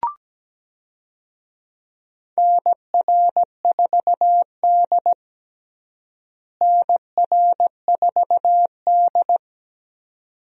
Morse code, heard as a keyed tone, about 17 wpm, 700 Hz. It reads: NR4D NR4D
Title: USA Call Signs - 4 Characters - Repeated 2x - 17wpm